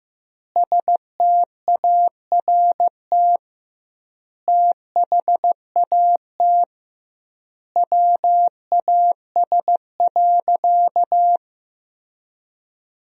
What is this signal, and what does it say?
Morse code, heard as a keyed tone, about 15 wpm, 700 Hz. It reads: START THAT WAS.
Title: 2x Word-Spacing - Sentences from Top 300 Words - 15wpm